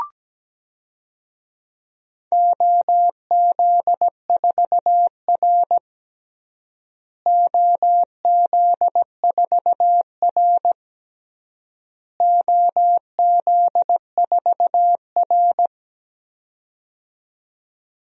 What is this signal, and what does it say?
Morse code, heard as a keyed tone, about 17 wpm, 700 Hz. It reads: OZ4R OZ4R OZ4R
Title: DXCC-100 Call Signs - 4 Characters - Repeated 3x - 17wpm